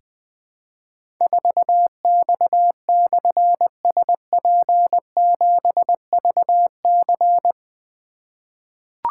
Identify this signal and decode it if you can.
Morse code, heard as a keyed tone, about 20 wpm, 700 Hz. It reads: 4X/SP7VC
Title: Difficult Call Signs 20wpm